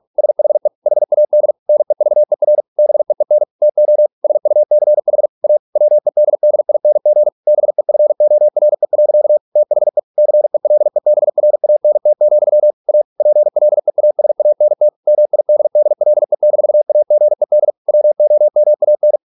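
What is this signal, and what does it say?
Morse code, heard as a keyed tone, about 48 wpm, 600 Hz. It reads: SHE HAD NEVER BEEN TO SUCH A WEDDING BEFORE. THE CELEBRANT, A PLEASANT MIDDLE-AGED WOMAN
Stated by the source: Morse practice at 45 wpm, the novel Never by Ken Follett in English, from CW med Gnister.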